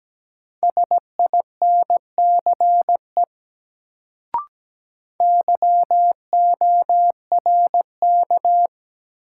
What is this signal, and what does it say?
Morse code, heard as a keyed tone, about 17 wpm, 700 Hz. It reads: SINCE YORK
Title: Sets of 2 Words - New Word Drill - 17wpm